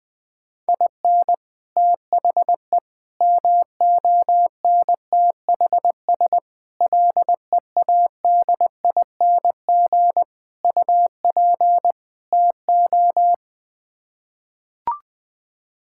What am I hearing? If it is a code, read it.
Morse code, heard as a keyed tone, about 20 wpm, 700 Hz. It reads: IN THE MONTHS LEADING UP TO
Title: Sets of 6 Words - 20wpm